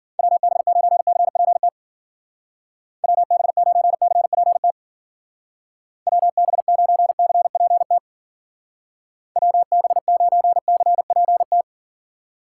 Morse code, heard as a keyed tone, about 48 wpm, 700 Hz. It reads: WB9CPT WB9CPT WB9CPT WB9CPT
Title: Speed Racing - USA 2x3 Call Signs 40wpm